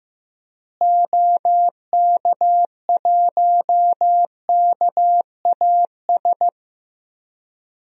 Morse code, heard as a keyed tone, about 15 wpm, 700 Hz. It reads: OK1KAS
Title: DXCC-100 Call Signs - 6 Characters - Repeated 3x - 15wpm